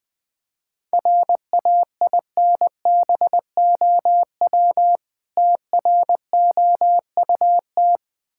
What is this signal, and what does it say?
Morse code, heard as a keyed tone, about 20 wpm, 690 Hz. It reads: RAINBOW TROUT
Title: Difficult Words with Hints - 20wpm